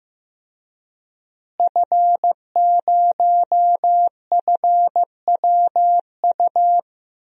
Morse code, heard as a keyed tone, about 15 wpm, 700 Hz. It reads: F0FWU
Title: DXCC-100 Call Signs - 5 Characters - Repeated 2x - 15wpm